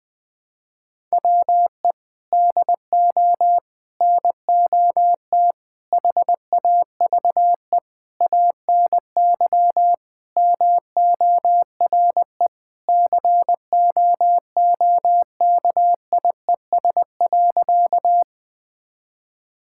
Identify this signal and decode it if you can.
Morse code, heard as a keyed tone, about 20 wpm, 700 Hz. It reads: WE DO NOT HAVE ANY MORE COOKIES.
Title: Sentences with 5-Word Phrases - 20wpm